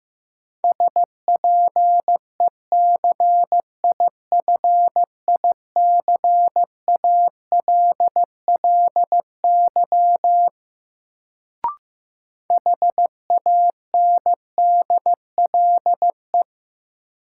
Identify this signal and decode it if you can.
Morse code, heard as a keyed tone, about 15 wpm, 700 Hz. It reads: SPECIFICALLY HANDLE
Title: Sentences from Top 1100 Words - New Word Drill Alt - 15wpm